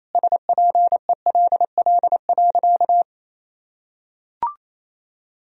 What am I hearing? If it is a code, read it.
Morse code, heard as a keyed tone, about 28 wpm, 700 Hz. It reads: SPELL.
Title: Sentences from Top 1000 Words - Review Part 1 - 28wpm